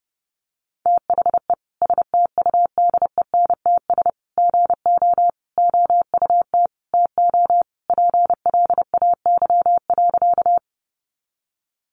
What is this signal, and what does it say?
Morse code, heard as a keyed tone, about 30 wpm, 700 Hz. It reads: THE STUDENTS GO OUT TO PLAY.
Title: Easy Phrases 30wpm